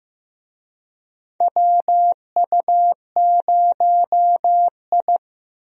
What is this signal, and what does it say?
Morse code, heard as a keyed tone, about 15 wpm, 700 Hz. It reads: WU0I
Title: USA Call Signs - 4 Characters - Repeated 3x - 15wpm